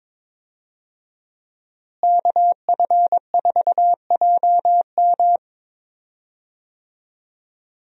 Morse code, heard as a keyed tone, about 22 wpm, 700 Hz. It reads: KF4JM
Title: USA Call Signs - 5 Characters - Repeated 3x - 22wpm